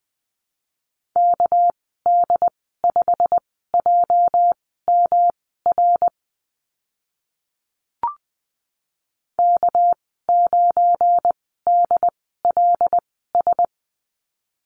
Morse code, heard as a keyed tone, about 20 wpm, spaced as for 14 wpm, 700 Hz. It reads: KD5JMR K9DLS
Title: 245 Lesson 39g - Letter z - Call Signs 20wpm